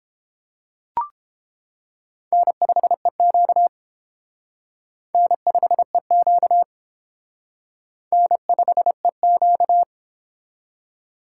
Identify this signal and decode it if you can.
Morse code, heard as a keyed tone, about 29 wpm, 700 Hz. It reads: N5EQ N5EQ N5EQ